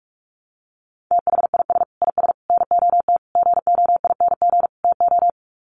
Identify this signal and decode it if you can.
Morse code, heard as a keyed tone, about 44 wpm, 700 Hz. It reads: THIS IS NOT GOING TO
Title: Sets of 5 Words Encore2 45wpm